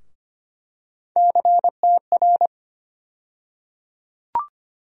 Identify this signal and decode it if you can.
Morse code, heard as a keyed tone, about 25 wpm, 700 Hz. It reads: CTR
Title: USA Call Sign Postfixes - 3 Letters 25wpm